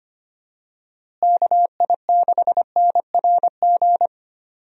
Morse code, heard as a keyed tone, about 25 wpm, 700 Hz. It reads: KI6NRG
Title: USA Call Signs - 6 Characters - Repeated 3x - 25wpm